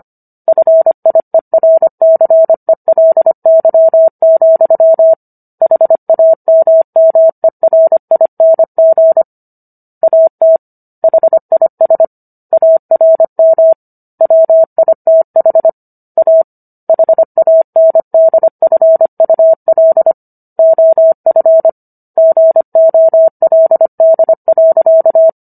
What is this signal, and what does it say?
Morse code, heard as a keyed tone, about 25 wpm, 650 Hz. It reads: FIERCELY, HAMMERING AT HIS ARM WITH A HANDFUL OF GOLD.